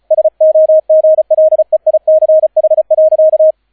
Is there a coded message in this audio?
Yes: Morse code from 0.1 to 3.5 s, heard as a keyed tone, about 36 wpm, 630 Hz, with radio noise. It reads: SOGLEICH.